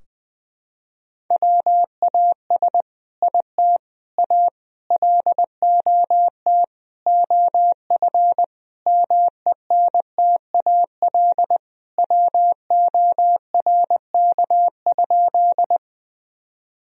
Morse code, heard as a keyed tone, about 20 wpm, 700 Hz. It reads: WAS IT A LOT OF MENTAL WORK?
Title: Sentences from Top 1000 Words - Review Part 4 - 20wpm